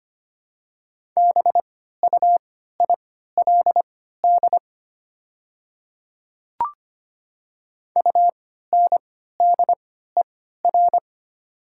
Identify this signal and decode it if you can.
Morse code, heard as a keyed tone, about 25 wpm, spaced as for 14 wpm, 700 Hz. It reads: BUILD UNDER
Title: 3x Character Spacing - Top 200 Words - 25wpm